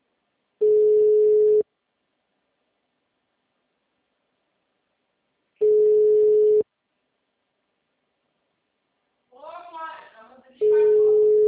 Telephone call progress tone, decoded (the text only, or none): ringback tone